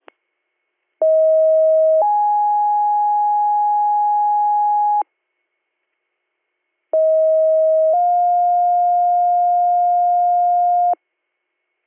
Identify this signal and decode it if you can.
Two-tone page, heard as a one-second tone, then a three-second tone, about 640 then 830 Hz, about 640 then 710 Hz. It is heard from 1.0 to 10.9 s.